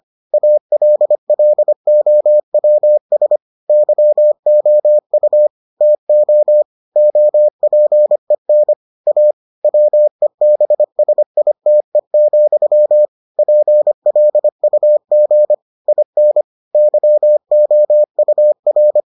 Morse code, heard as a keyed tone, about 25 wpm, 600 Hz. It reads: ALLOWS YOU TO OPEN A WEBSITE, PLUG IN YOUR